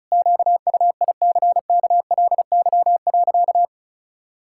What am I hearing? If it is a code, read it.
Morse code, heard as a keyed tone, about 36 wpm, 700 Hz. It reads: QUICKLY.
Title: Pangram Challenge v2 - 35wpm